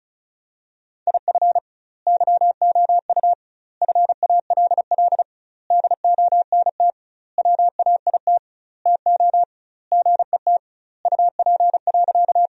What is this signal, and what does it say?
Morse code, heard as a keyed tone, about 36 wpm, 700 Hz. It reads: IF YOU FALL DONT WAIT TO GET UP.
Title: 2x Word-Spacing - Sentences from Top 500 Words - 35wpm